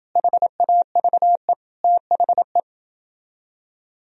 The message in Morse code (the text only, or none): HAVE THE